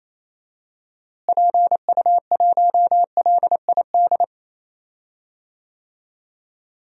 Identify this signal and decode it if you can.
Morse code, heard as a keyed tone, about 28 wpm, 700 Hz. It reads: PU1LID
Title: DXCC-100 Call Signs - 6 Characters - Repeated 3x - 28wpm